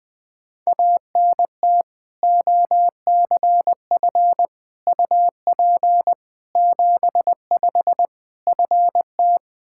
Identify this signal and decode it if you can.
Morse code, heard as a keyed tone, about 21 wpm, 700 Hz. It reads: ANT OCF UP 75 FT
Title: Practice QSO - Exchange 2 - Info - 20wpm